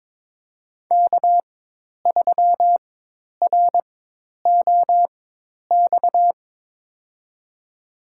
Morse code, heard as a keyed tone, about 21 wpm, spaced as for 10 wpm, 700 Hz. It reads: K3ROX
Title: USA Call Signs - 5 Characters - 4x Character Spacing - 22wpm